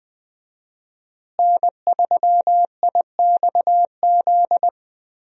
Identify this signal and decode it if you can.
Morse code, heard as a keyed tone, about 20 wpm, 700 Hz. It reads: N3IXZ
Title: USA Call Signs - 5 Characters - Repeated 2x - 20wpm